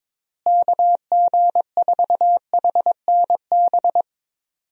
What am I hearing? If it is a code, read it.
Morse code, heard as a keyed tone, about 21 wpm, 700 Hz. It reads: KG4HNB
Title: USA Call Signs - 6 Characters - Repeated 3x - 22wpm